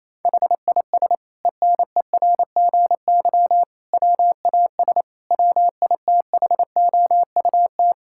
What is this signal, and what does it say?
Morse code, heard as a keyed tone, about 28 wpm, 700 Hz. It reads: HIS ENERGY WAS WITHOUT
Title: Sentences from Top 1000 Words - Review Part 6 - 28wpm